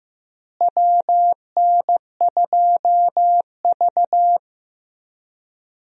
Morse code, heard as a keyed tone, about 15 wpm, 700 Hz. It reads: WN2V
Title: USA Call Signs - 4 Characters - Repeated 3x - 15wpm